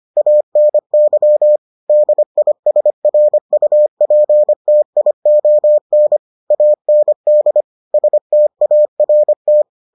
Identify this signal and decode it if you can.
Morse code, heard as a keyed tone, about 25 wpm, 600 Hz. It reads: ANY DISRUPTION AND START